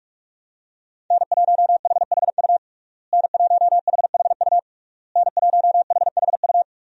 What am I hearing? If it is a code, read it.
Morse code, heard as a keyed tone, about 46 wpm, 700 Hz. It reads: N1HHU N1HHU N1HHU
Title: USA Call Signs - 5 Characters - Repeated 3x - 45wpm